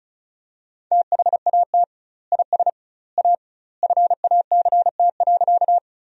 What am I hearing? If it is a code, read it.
Morse code, heard as a keyed tone, about 34 wpm, 700 Hz. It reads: THAT IS A FACT.
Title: 2x Word-Spacing - Sentences from Top 500 Words - 35wpm